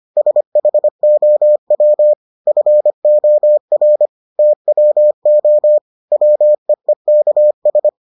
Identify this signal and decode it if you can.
Morse code, heard as a keyed tone, about 25 wpm, 600 Hz. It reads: SHOW FOR TWO WEEKS